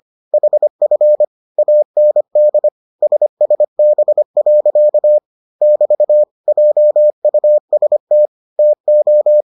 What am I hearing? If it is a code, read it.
Morse code, heard as a keyed tone, about 25 wpm, 600 Hz. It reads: HF AND SSB. <BT> JUST TO